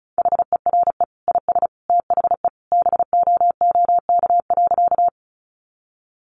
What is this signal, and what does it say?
Morse code, heard as a keyed tone, about 34 wpm, 700 Hz. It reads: HERE IS THE BOOK.
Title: Sentences from Top 300 Words 35wpm